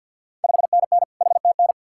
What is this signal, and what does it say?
Morse code, heard as a keyed tone, about 50 wpm, 700 Hz. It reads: 5NN HTN
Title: Ontario QSO Party Exchange 50wpm